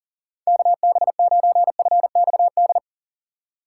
Morse code, heard as a keyed tone, about 40 wpm, 700 Hz. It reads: KB9FXD
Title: USA Call Signs - 3x Character Spacing - 40wpm